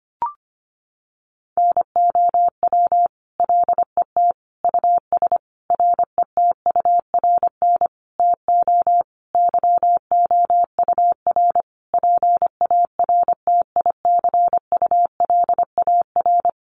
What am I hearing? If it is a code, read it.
Morse code, heard as a keyed tone, about 25 wpm, 700 Hz. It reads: NOW LET US RETURN TO YOUR PARTICULAR
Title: Sentences from Top 800 Words 25wpm